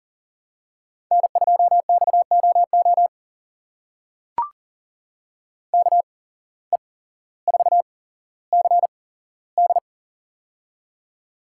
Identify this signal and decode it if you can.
Morse code, heard as a keyed tone, about 40 wpm, spaced as for 17 wpm, 700 Hz. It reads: N2XOO KE4CD
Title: USA Call Signs - 5 Characters - 8x Character Spacing - 40wpm